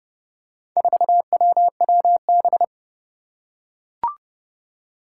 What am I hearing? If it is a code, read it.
Morse code, heard as a keyed tone, about 30 wpm, 700 Hz. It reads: 4WWB
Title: Four Challenging Letter-Number Combinations 30wpm